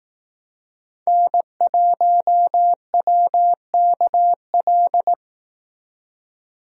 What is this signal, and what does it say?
Morse code, heard as a keyed tone, about 18 wpm, 700 Hz. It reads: N1WKL
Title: Speed Racing - USA 1x3 Call Signs 15wpm